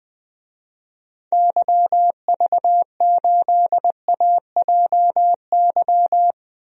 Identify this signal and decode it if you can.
Morse code, heard as a keyed tone, about 20 wpm, 700 Hz. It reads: YV8AJY
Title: DXCC-100 Call Signs - 6 Characters - Repeated 3x - 20wpm